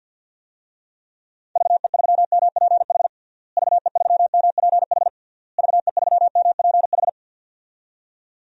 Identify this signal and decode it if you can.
Morse code, heard as a keyed tone, about 51 wpm, 700 Hz. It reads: VE3MPH VE3MPH VE3MPH
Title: DXCC-100 Call Signs - 6 Characters - Repeated 3x - 50wpm